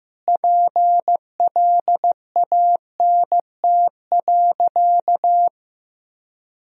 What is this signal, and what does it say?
Morse code, heard as a keyed tone, about 15 wpm, 700 Hz. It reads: PLANT.
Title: Sentences from Top 1000 Words - Review Part 1 - 15wpm